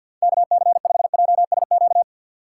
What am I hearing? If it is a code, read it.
Morse code, heard as a keyed tone, about 48 wpm, 700 Hz. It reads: K C 5 J S Q
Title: USA Call Signs - 6x Character Spacing - 50wpm